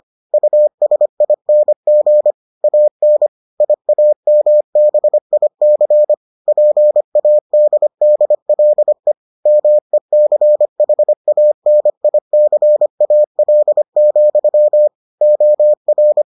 Morse code, heard as a keyed tone, about 25 wpm, 600 Hz. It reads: USING AN IAMBIC PADDLE MECHANICAL, OR